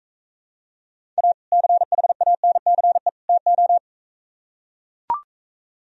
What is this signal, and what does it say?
Morse code, heard as a keyed tone, about 45 wpm, 700 Hz. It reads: A CHANCE TO